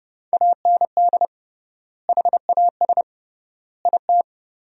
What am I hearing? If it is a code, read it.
Morse code, heard as a keyed tone, about 32 wpm, 700 Hz. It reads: AND HAS IT
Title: 3x Word-Spacing - Sentences from Top 100 Words - 30wpm